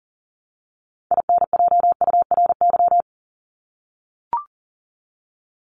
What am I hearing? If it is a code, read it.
Morse code, heard as a keyed tone, about 40 wpm, 700 Hz. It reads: INJURY
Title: Top 1751-2000 Words 40wpm